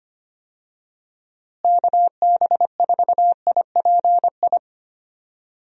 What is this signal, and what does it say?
Morse code, heard as a keyed tone, about 25 wpm, 700 Hz. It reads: KB4IPI
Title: USA Call Signs - 6 Characters - Repeated 2x - 25wpm